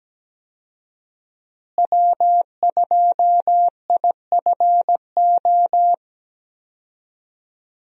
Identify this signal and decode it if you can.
Morse code, heard as a keyed tone, about 17 wpm, 700 Hz. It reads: W2IFO
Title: USA Call Signs - 5 Characters - Repeated 2x - 17wpm